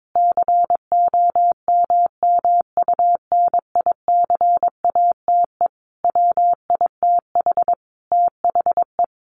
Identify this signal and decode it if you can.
Morse code, heard as a keyed tone, about 22 wpm, 700 Hz. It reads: COMMUNICATE WITH THE
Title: Sentences from Top 1300 Words 22wpm